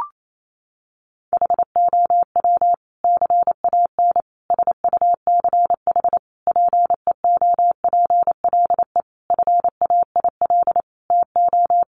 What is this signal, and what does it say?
Morse code, heard as a keyed tone, about 28 wpm, 700 Hz. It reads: HOW CAN SUCH PEOPLE FAIL TO
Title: Sentences from Top 1100 Words 28wpm